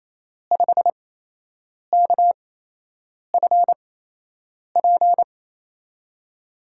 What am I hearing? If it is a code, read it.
Morse code, heard as a keyed tone, about 28 wpm, spaced as for 8 wpm, 700 Hz. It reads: 5KFP
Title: USA Call Signs - 5 Characters - 8x Character Spacing - 28wpm